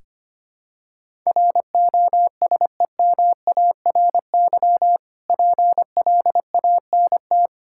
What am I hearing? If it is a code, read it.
Morse code, heard as a keyed tone, about 25 wpm, 690 Hz. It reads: ROSEMARY PLANT
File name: practice / 25wpm / Difficult Words with Hints - 25wpm